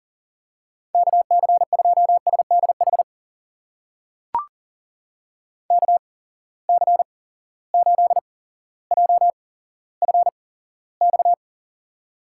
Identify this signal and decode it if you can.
Morse code, heard as a keyed tone, about 40 wpm, spaced as for 17 wpm, 700 Hz. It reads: KC2SDH KC8JFX
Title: USA Call Signs - 8x Character Spacing - 40wpm